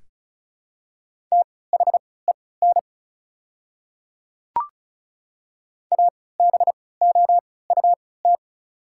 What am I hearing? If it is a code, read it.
Morse code, heard as a keyed tone, about 34 wpm, spaced as for 20 wpm, 700 Hz. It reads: THEN ABOUT
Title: 3x Character Spacing - Top 200 Words - 35wpm